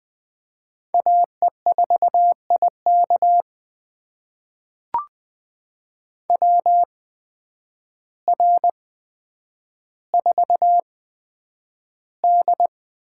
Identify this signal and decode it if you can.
Morse code, heard as a keyed tone, about 21 wpm, spaced as for 9 wpm, 700 Hz. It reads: AE4IK WR4D
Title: USA Call Signs - 8x Character Spacing - 20wpm